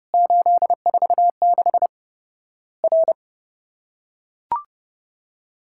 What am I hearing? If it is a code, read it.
Morse code, heard as a keyed tone, about 30 wpm, 700 Hz. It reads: 846 R
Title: Worked All Europe DX Contest - QTC Exchange - 30wpm